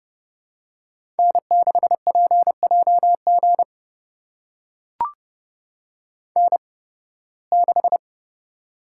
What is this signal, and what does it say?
Morse code, heard as a keyed tone, about 30 wpm, spaced as for 20 wpm, 700 Hz. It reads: N6PJG N6
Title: USA Call Signs - 5 Characters - 8x Character Spacing - 30wpm